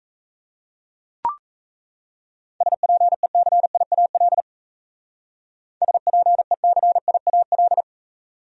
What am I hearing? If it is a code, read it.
Morse code, heard as a keyed tone, about 36 wpm, 700 Hz. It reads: SPECIAL SPECIAL